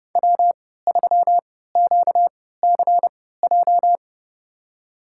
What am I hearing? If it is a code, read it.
Morse code, heard as a keyed tone, about 30 wpm, spaced as for 17 wpm, 700 Hz. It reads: W3QCJ
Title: USA Call Signs - 5 Characters - 3x Character Spacing - 30wpm